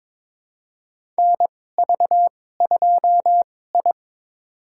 Morse code, heard as a keyed tone, about 22 wpm, spaced as for 16 wpm, 700 Hz. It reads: NV2I
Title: USA Call Signs - 4 Characters - 2x Character Spacing - 22wpm